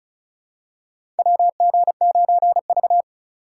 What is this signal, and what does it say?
Morse code, heard as a keyed tone, about 36 wpm, 700 Hz. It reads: WG9V
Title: USA Call Signs - 4 Characters - 4x Character Spacing - 35wpm